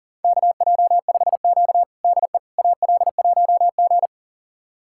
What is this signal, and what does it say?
Morse code, heard as a keyed tone, about 37 wpm, 700 Hz. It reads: KJ5Q DE AL1G
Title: USA Callsign DE Callsign - 4 Characters - 40wpm